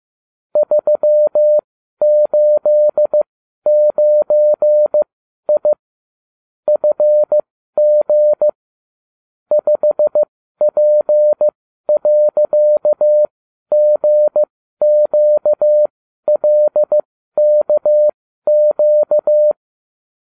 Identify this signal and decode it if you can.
Morse code, heard as a keyed tone, about 15 wpm, spaced as for 12 wpm, 600 Hz. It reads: 389I FG 5P.GQLKQ